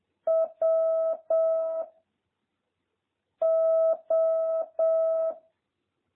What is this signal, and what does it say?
Morse code, heard as a keyed tone, about 7 wpm, 650 Hz. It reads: W O